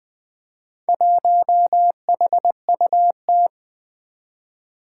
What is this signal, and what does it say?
Morse code, heard as a keyed tone, about 21 wpm, 700 Hz. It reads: 1HUT
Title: Four Challenging Letter-Number Combinations 20wpm